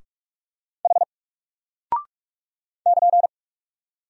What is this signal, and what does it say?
Morse code, heard as a keyed tone, about 44 wpm, 700 Hz. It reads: H <KN>
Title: Single Character (Letter:Number:Punctuation:Prosign) Alt 45wpm